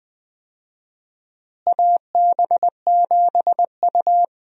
Morse code, heard as a keyed tone, about 20 wpm, 700 Hz. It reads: AB7U
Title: USA Call Signs - 4 Characters - Repeated 3x - 20wpm